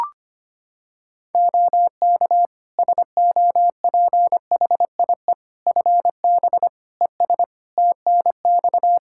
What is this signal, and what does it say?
Morse code, heard as a keyed tone, about 25 wpm, 700 Hz. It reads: OK SOPHIE FB ES TNX
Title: Practice QSO - Exchange 2 - Info - 25wpm